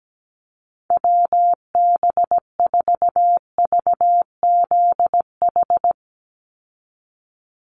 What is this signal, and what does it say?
Morse code, heard as a keyed tone, about 17 wpm, 700 Hz. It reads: WB4VZH
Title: USA Call Signs 17wpm